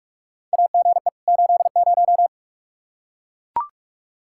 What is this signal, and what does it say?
Morse code, heard as a keyed tone, about 43 wpm, 700 Hz. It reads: AGE 80
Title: Practice QSO - Exchange 3 - More Info - 45wpm